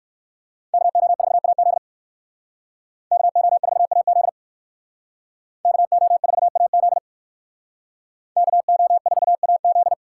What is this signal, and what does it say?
Morse code, heard as a keyed tone, about 58 wpm, 700 Hz. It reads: KO4AZ KO4AZ KO4AZ KO4AZ